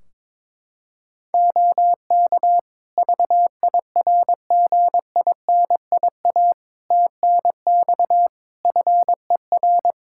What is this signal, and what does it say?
Morse code, heard as a keyed tone, about 22 wpm, 700 Hz. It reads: OK VIRGINIA TNX FER
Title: Practice QSO - Exchange 4 - Ending - 22wpm